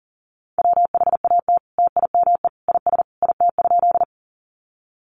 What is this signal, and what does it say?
Morse code, heard as a keyed tone, about 40 wpm, 700 Hz. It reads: WHAT TIME IS IT?
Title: Sentences from Top 100 Words 40wpm